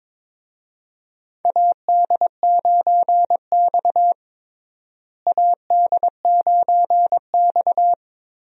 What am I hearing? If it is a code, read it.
Morse code, heard as a keyed tone, about 22 wpm, 700 Hz. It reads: AD9X AD9X
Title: USA Call Signs - 4 Characters - Repeated 2x - 22wpm